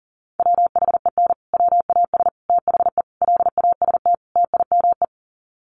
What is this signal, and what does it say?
Morse code, heard as a keyed tone, about 40 wpm, 700 Hz. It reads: WHEN WAS THE LAST TIME
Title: Sets of 5 Words Encore 40wpm